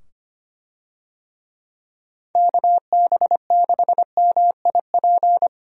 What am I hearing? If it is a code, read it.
Morse code, heard as a keyed tone, about 25 wpm, 700 Hz. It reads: KB6MIP
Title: USA Call Signs - 6 Characters - 25wpm